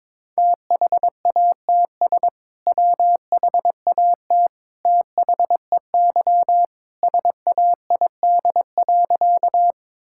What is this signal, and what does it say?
Morse code, heard as a keyed tone, about 23 wpm, 700 Hz. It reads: THATS WHAT THEY SAID.
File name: practice / 22wpm / Sentences with 2-Word Phrases - 22wpm